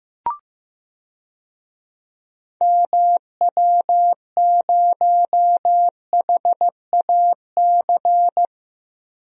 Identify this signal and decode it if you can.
Morse code, heard as a keyed tone, about 15 wpm, 700 Hz. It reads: MW0HAC
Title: DXCC-100 Call Signs - 6 Characters - 15wpm